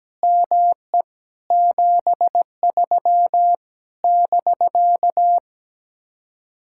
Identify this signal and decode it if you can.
Morse code, heard as a keyed tone, about 17 wpm, 700 Hz. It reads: ME 73 <BK>
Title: Speed-Racing Common QSO Phrases Encore - 17wpm